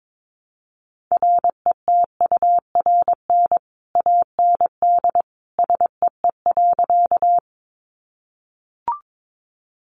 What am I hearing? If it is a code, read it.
Morse code, heard as a keyed tone, about 22 wpm, 700 Hz. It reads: RETURN AND SEE.
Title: Sentences from Top 800 Words 22wpm